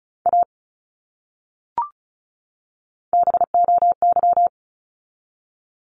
Morse code, heard as a keyed tone, about 34 wpm, 700 Hz. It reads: A BOY
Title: Top 200 Words 35wpm